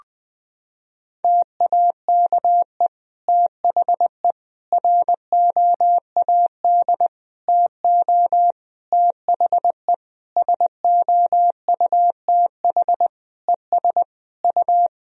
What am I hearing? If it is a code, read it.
Morse code, heard as a keyed tone, about 21 wpm, 700 Hz. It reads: TAKE THE ROAD TO THE SOUTH ES U
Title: Sentences from Top 400 Words and Abbreviations 20wpm